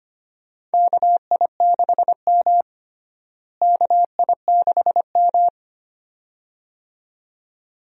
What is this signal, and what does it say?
Morse code, heard as a keyed tone, about 25 wpm, 700 Hz. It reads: KI6M KI6M
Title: USA Call Signs - 4 Characters - Repeated 2x - 25wpm